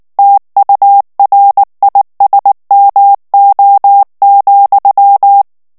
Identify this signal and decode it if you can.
Morse code, heard as a keyed tone, about 19 wpm, 800 Hz. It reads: TURISMO,
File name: morse mp3 / ESPAGNOL TEXTE EN CLAIR 2 VITESSE 18 MOTS MINUTES